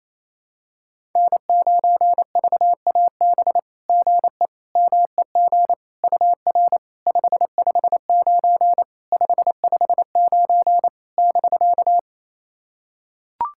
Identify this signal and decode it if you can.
Morse code, heard as a keyed tone, about 27 wpm, 690 Hz. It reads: N9VAB GE MEG UR 559 559 <BK>
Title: SOTA QSO Practice 28wpm